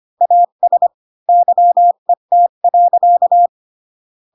Morse code, heard as a keyed tone, about 25 wpm, 700 Hz. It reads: AS YET.
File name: Pinocchio / Pinocchio ch5 20 25 ik1hge